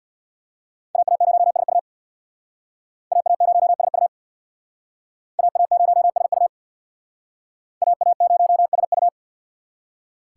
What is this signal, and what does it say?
Morse code, heard as a keyed tone, about 64 wpm, 700 Hz. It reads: AA0SU AA0SU AA0SU AA0SU